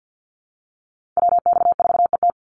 Morse code, heard as a keyed tone, about 50 wpm, 700 Hz. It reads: WX4ET
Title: NA SOTA Chaser Callsigns 50wpm